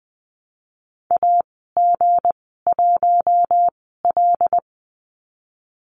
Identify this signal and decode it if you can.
Morse code, heard as a keyed tone, about 20 wpm, spaced as for 15 wpm, 700 Hz. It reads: AG1L
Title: 231 Lesson 37g - Letter x - Call Signs 20wpm